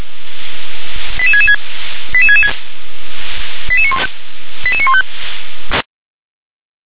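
Five-tone selective call (ZVEI), heard as quick runs of five tones, tones of about 70 ms. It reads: 88666 88666 88915
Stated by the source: ZVEI five-tone call received on SDR.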